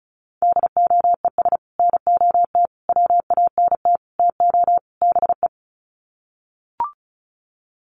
Morse code, heard as a keyed tone, about 34 wpm, 700 Hz. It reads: DOES NOT WANT TO BE
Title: Sets of 5 Words Encore 35wpm